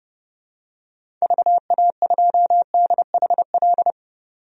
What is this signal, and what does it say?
Morse code, heard as a keyed tone, about 30 wpm, 700 Hz. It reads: VA2DHL
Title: DXCC-100 Call Signs - 6 Characters - 30wpm